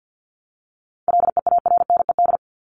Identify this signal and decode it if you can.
Morse code, heard as a keyed tone, about 50 wpm, 700 Hz. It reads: LEARNED
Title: ED Suffix - Top 100 Words with Base Words 50wpm